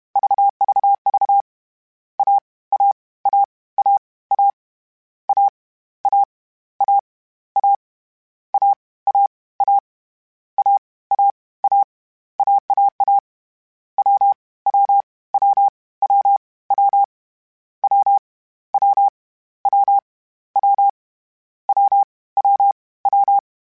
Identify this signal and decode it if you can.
Morse code, heard as a keyed tone, about 31 wpm, 800 Hz. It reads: VVV A A A A A A A A A A A A A A A AAA W W W W W W W W W W W W
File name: Short Runs / Short Runs-mp3s / AWJ @30